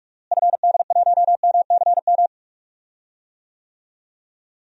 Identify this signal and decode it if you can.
Morse code, heard as a keyed tone, about 46 wpm, 700 Hz. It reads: FD1MCM
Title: DXCC-100 Call Signs - 6 Characters - Repeated 3x - 45wpm